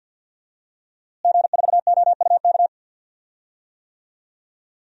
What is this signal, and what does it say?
Morse code, heard as a keyed tone, about 51 wpm, 700 Hz. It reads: G4OUK
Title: DXCC-100 Call Signs - 5 Characters - Repeated 3x - 50wpm